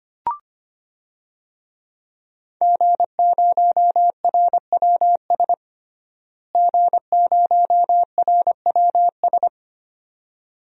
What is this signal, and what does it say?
Morse code, heard as a keyed tone, about 25 wpm, 700 Hz. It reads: G0RWS G0RWS
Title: DXCC-100 Call Signs - 5 Characters - Repeated 2x - 25wpm